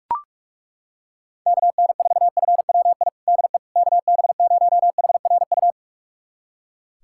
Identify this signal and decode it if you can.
Morse code, heard as a keyed tone, about 41 wpm, 700 Hz. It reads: KN4FWI DE KB0HRU
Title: USA Callsign DE Callsign - 6 Characters - 45wpm